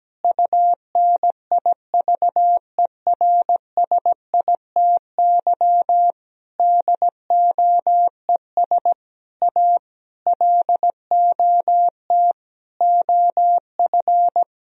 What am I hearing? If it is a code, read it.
Morse code, heard as a keyed tone, about 17 wpm, 700 Hz. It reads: UNIVERSITY DOES A LOT OF
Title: Sentences from Top 1000 Words - Review Part 4 - 17wpm